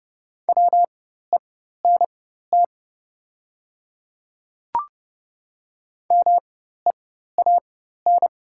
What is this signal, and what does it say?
Morse code, heard as a keyed tone, about 32 wpm, spaced as for 14 wpm, 700 Hz. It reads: WENT MEAN
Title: 4x Character Spacing - Top 200 Words - 30wpm